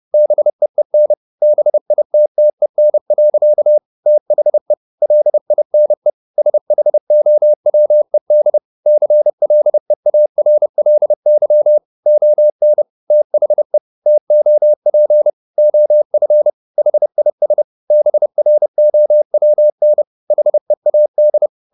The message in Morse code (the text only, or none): BEEN BITTEN. THE LINE SHOWED CLEARLY ON THE TOP OF HIS BROWN HEAD